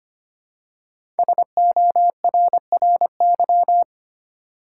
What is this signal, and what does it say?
Morse code, heard as a keyed tone, about 25 wpm, 700 Hz. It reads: SORRY
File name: 5 Letter Words - First 3 Letters Given - 25wpm